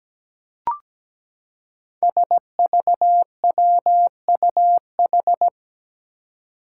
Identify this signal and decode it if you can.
Morse code, heard as a keyed tone, about 17 wpm, 700 Hz. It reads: SVWUH
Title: Five Challenging Letter-Number Combinations 17wpm